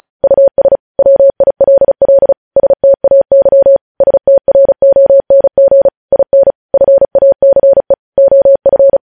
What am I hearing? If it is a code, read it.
Morse code, heard as a keyed tone, about 36 wpm, 550 Hz. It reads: US WILL STAY STRONG IN FACE OF
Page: News Headlines in Morse Code at 35 WPM